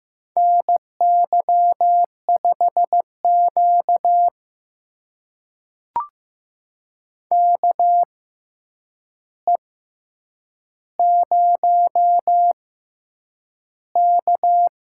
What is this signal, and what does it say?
Morse code, heard as a keyed tone, about 15 wpm, spaced as for 8 wpm, 700 Hz. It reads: NY5Q KE0K
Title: USA Call Signs - 6x Character Spacing - 15wpm